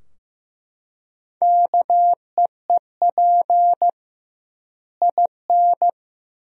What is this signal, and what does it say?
Morse code, heard as a keyed tone, about 15 wpm, 700 Hz. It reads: KEEP IN
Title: 2x Word-Spacing - Sentences from Top 400 Words - 15wpm